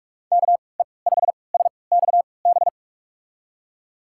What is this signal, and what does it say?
Morse code, heard as a keyed tone, about 46 wpm, spaced as for 25 wpm, 690 Hz. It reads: KE5SXB
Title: USA Call Signs - 6 Characters - 3x Character Spacing - 45wpm